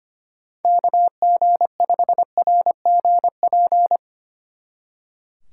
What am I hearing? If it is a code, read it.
Morse code, heard as a keyed tone, about 30 wpm, 700 Hz. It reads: KG5RGP
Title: Speed Racing - USA 2x3 Call Signs 25wpm